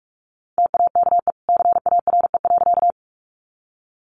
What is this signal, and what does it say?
Morse code, heard as a keyed tone, about 44 wpm, 700 Hz. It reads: TAKE CARE.
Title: Sentences from Top 300 Words 45wpm